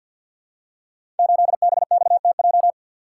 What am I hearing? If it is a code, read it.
Morse code, heard as a keyed tone, about 50 wpm, 700 Hz. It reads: 8BXTJ
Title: Code Groups - 5 Letter:Numbers - 50wpm